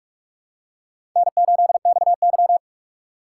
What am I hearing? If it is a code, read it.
Morse code, heard as a keyed tone, about 46 wpm, 700 Hz. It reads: N8XY
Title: USA Call Signs - 4 Characters - 3x Character Spacing - 45wpm